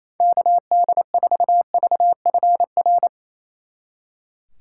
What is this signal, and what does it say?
Morse code, heard as a keyed tone, about 35 wpm, 700 Hz. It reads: KD4VFR
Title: Speed Racing - USA 2x3 Call Signs 28wpm